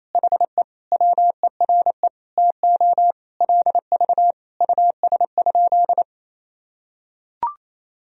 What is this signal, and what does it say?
Morse code, heard as a keyed tone, about 28 wpm, 700 Hz. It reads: HE WERE TO LV US?
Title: Sentences from Top 400 Words and Abbreviations 28wpm